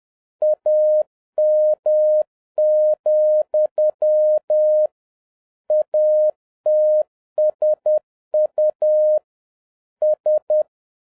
Morse code, heard as a keyed tone, about 10 wpm, 610 Hz. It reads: AM, ATSU S
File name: 2011-07-mp3 / n46 JOHN K3TN 10wpm